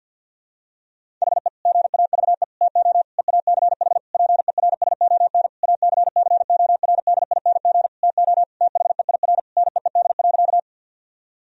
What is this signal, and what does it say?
Morse code, heard as a keyed tone, about 50 wpm, 700 Hz. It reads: HE GAVE TO EACH PERSON ACCORDING TO THEIR NEED.
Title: Sentences from Top 1000 Words - Review Part 4 - 50wpm